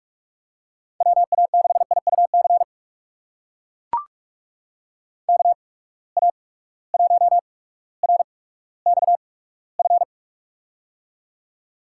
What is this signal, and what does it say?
Morse code, heard as a keyed tone, about 42 wpm, spaced as for 19 wpm, 700 Hz. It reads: WA6IUC KA1RXF
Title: USA Call Signs - 6 Characters - 8x Character Spacing - 45wpm